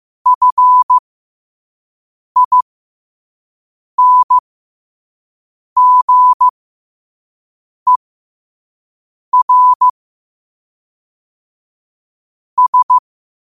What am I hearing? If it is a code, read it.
Morse code, heard as a keyed tone, about 15 wpm, spaced as for 5 wpm, 1000 Hz. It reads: FINGER S
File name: 5wpm